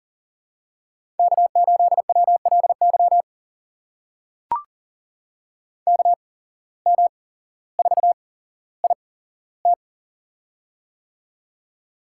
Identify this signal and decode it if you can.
Morse code, heard as a keyed tone, about 38 wpm, spaced as for 17 wpm, 700 Hz. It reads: K8WLY KM4IT
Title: USA Call Signs - 5 Characters - 8x Character Spacing - 40wpm